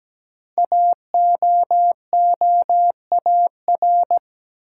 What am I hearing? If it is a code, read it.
Morse code, heard as a keyed tone, about 17 wpm, 700 Hz. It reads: AOOAR
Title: Code Groups - 5 Letter:Numbers - 17wpm